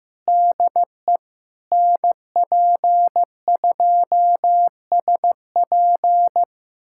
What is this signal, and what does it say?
Morse code, heard as a keyed tone, about 16 wpm, 700 Hz. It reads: DE NP2SP
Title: USA Callsign DE Callsign - 5 Characters - 15wpm